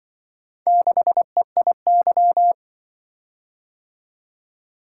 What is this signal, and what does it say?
Morse code, heard as a keyed tone, about 25 wpm, 700 Hz. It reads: 6EIY
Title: Speed-Racing - Four Letter-Number Combinations - 20wpm